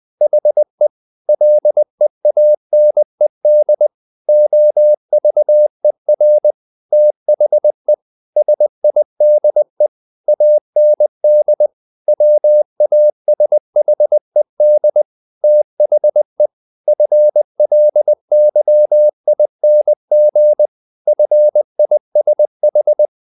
Morse code, heard as a keyed tone, about 20 wpm, 600 Hz. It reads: HE LEANED OVER THE SIDE AND WASHED THE FLYING FISH
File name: the old man and the sea part 2 20wpm